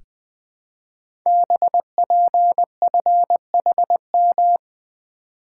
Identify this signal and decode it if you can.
Morse code, heard as a keyed tone, about 20 wpm, 700 Hz. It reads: BPFHM
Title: Code Groups - 5 Letters - 20wpm